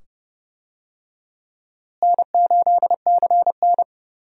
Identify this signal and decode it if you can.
Morse code, heard as a keyed tone, about 30 wpm, 700 Hz. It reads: N8CN
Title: USA Call Signs - 4 Characters - Repeated 2x - 30wpm